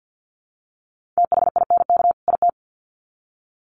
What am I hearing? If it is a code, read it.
Morse code, heard as a keyed tone, about 50 wpm, 700 Hz. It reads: THINK IT
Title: Sets of 2 Words 50wpm